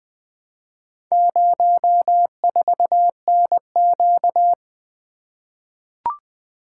Morse code, heard as a keyed tone, about 20 wpm, 700 Hz. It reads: 04NQ
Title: Four Letter-Number Combinations - 20wpm